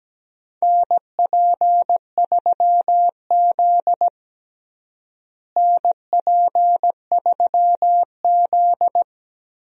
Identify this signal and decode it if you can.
Morse code, heard as a keyed tone, about 17 wpm, 700 Hz. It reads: NP3Z NP3Z
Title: USA Call Signs - 4 Characters - Repeated 3x - 17wpm